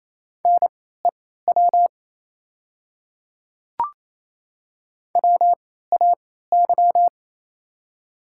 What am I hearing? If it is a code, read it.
Morse code, heard as a keyed tone, about 28 wpm, spaced as for 16 wpm, 700 Hz. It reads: NEW WAY
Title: 3x Character Spacing - Top 300 Words - 28wpm